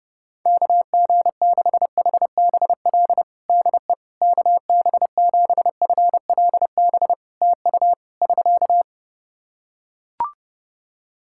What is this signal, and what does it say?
Morse code, heard as a keyed tone, about 30 wpm, 700 Hz. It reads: KG6HBL DE KB7FLB TU <SK>
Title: Practice QSO - Exchange 4 - Ending - 30wpm